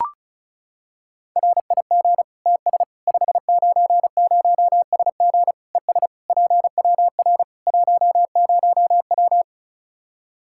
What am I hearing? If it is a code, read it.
Morse code, heard as a keyed tone, about 36 wpm, 700 Hz. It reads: RIG TS 590SG ES PWR 10W
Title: Practice QSO - Exchange 2 - Info - 35wpm